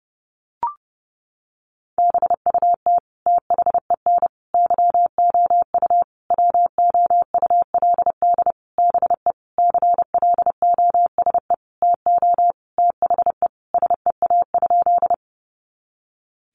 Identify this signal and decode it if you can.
Morse code, heard as a keyed tone, about 30 wpm, 700 Hz. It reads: BUT THEN YOU WOULD BE CLOSE TO THE SEA?
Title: Sentences from Top 300 Words 30wpm